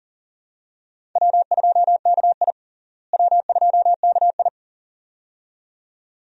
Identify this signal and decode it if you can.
Morse code, heard as a keyed tone, about 40 wpm, 700 Hz. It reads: W2KI W2KI
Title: USA Call Signs - 4 Characters - Repeated 2x - 40wpm